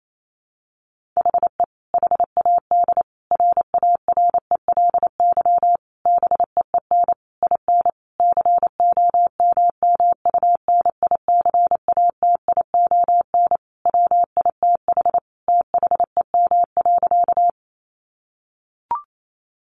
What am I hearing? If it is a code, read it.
Morse code, heard as a keyed tone, about 28 wpm, 700 Hz. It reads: HE HAD RARELY BEEN IN COMMUNICATION WITH THEM.
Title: Sentences from Top 1400 Words 28wpm